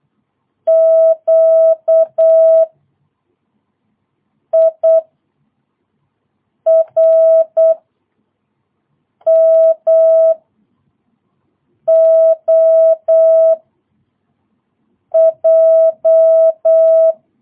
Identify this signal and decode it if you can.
Morse code, heard as a keyed tone, about 8 wpm, 660 Hz. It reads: Q I R M O J
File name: jh8cbh / 2022cw